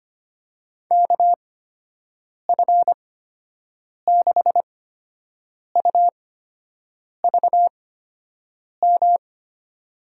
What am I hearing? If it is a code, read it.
Morse code, heard as a keyed tone, about 25 wpm, spaced as for 7 wpm, 700 Hz. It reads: KF6UVM
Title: USA Call Signs - 6 Characters - 8x Character Spacing - 25wpm